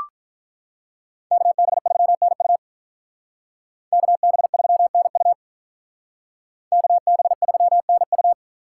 Morse code, heard as a keyed tone, about 44 wpm, 700 Hz. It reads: KB3NU KB3NU KB3NU
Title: Speed Racing - USA 2x2 Call Signs 35wpm